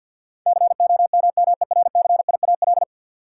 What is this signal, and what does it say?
Morse code, heard as a keyed tone, about 49 wpm, 700 Hz. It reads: COMMERCIAL
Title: Sentences from Top 1000 Words - Review Part 5 - New Word Drill - 50wpm